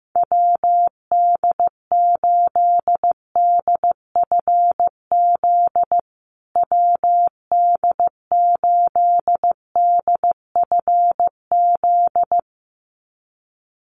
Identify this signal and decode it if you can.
Morse code, heard as a keyed tone, about 15 wpm, 700 Hz. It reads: WD8DFZ WD8DFZ
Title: USA Call Signs Repeated 2x 15wpm